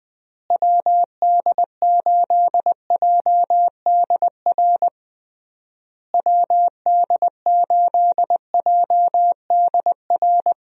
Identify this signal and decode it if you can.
Morse code, heard as a keyed tone, about 20 wpm, 700 Hz. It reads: WD8JDR WD8JDR
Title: USA Call Signs - 6 Characters - Repeated 3x - 20wpm